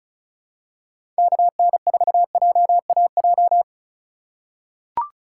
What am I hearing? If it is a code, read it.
Morse code, heard as a keyed tone, about 34 wpm, 700 Hz. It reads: KN4JAJ
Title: USA Call Signs - 6 Characters - 3x Character Spacing - 35wpm